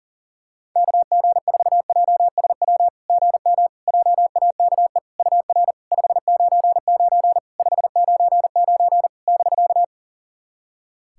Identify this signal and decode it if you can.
Morse code, heard as a keyed tone, about 40 wpm, 700 Hz. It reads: KG4JSW GM JAKE UR 599 599 <BK>
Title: SOTA QSO Practice 40wpm